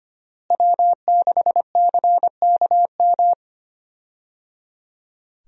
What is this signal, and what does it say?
Morse code, heard as a keyed tone, about 25 wpm, 700 Hz. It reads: W6CKM
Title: USA Call Signs - 5 Characters - Repeated 3x - 25wpm